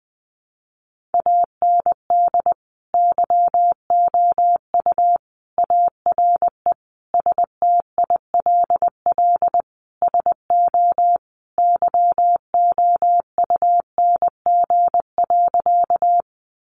Morse code, heard as a keyed tone, about 20 wpm, 700 Hz. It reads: AND YOU ARE STILL SO YOUNG.
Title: Sentences from Top 400 Words 20wpm